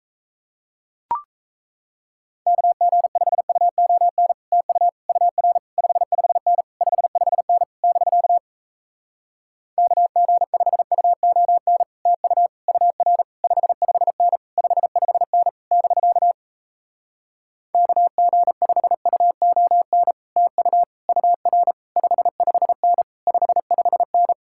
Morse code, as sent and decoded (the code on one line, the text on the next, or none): -.- --. ..... ..- --- -. / - ..- / ..- .-. / ..... ..... -. / ..... ..... -. / -...-.- / -.- --. ..... ..- --- -. / - ..- / ..- .-. / ..... ..... -. / ..... ..... -. / -...-.- / -.- --. ..... ..- --- -. / - ..- / ..- .-. / ..... ..... -. / ..... ..... -.
KG5UON TU UR 55N 55N <BK> KG5UON TU UR 55N 55N <BK> KG5UON TU UR 55N 55N